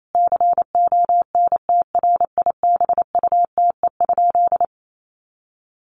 Morse code, heard as a keyed tone, about 28 wpm, 700 Hz. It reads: CONTRIBUTE?
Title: Sentences from Top 1400 Words 28wpm